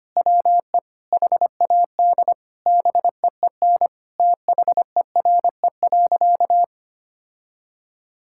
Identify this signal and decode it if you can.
Morse code, heard as a keyed tone, about 25 wpm, 700 Hz. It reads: WE HAD BEEN THERE.
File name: practice / 25wpm / Sentences with 4-Word Phrases - 25wpm